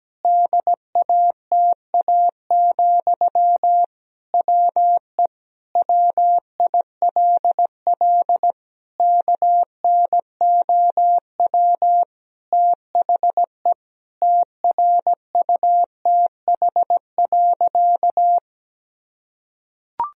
Morse code, heard as a keyed tone, about 17 wpm, 700 Hz. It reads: DATA, WE WILL KNOW THE TRUTH.
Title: Sentences from Top 1000 Words - Review Part 3 - 17wpm